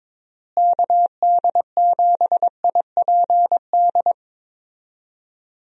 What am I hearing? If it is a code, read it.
Morse code, heard as a keyed tone, about 22 wpm, 700 Hz. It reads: KD7IPD
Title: USA Call Signs - 6 Characters - Repeated 2x - 22wpm